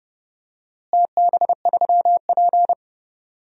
Morse code, heard as a keyed tone, about 30 wpm, 700 Hz. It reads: TB3P
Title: DXCC-100 Call Signs - 4 Characters - Repeated 3x - 30wpm